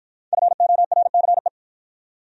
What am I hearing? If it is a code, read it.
Morse code, heard as a keyed tone, about 53 wpm, 700 Hz. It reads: FORCE